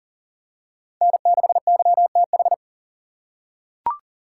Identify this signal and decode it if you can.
Morse code, heard as a keyed tone, about 40 wpm, 700 Hz. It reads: N6YTH
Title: USA Call Signs - 5 Characters - 2x Character Spacing - 40wpm